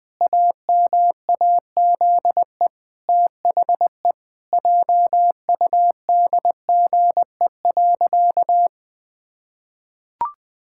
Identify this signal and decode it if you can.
Morse code, heard as a keyed tone, about 21 wpm, 700 Hz. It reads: AMAZE THE JUDGE.
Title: Pangram Challenge v2 - 20wpm